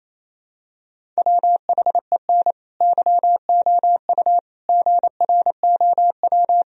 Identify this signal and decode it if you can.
Morse code, heard as a keyed tone, about 28 wpm, 700 Hz. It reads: WHEN YOU GROW
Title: Sentences from Top 1000 Words - Review Part 1 - 28wpm